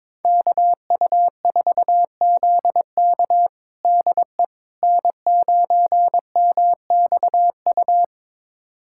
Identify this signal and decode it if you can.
Morse code, heard as a keyed tone, about 22 wpm, 700 Hz. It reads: KU4ZK DE N9MXU
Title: USA Callsign DE Callsign - 5 Characters - 22wpm